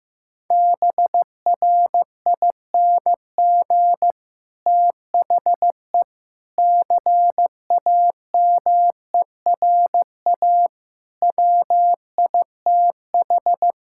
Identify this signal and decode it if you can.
Morse code, heard as a keyed tone, about 14 wpm, 700 Hz. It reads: BRING THE CAMERA WITH
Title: Sentences from Top 1000 Words - Review Part 5 - 15wpm